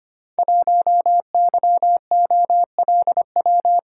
Morse code, heard as a keyed tone, about 25 wpm, 700 Hz. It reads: 1YOLW
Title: Code Groups - 5 Letter:Numbers - 25wpm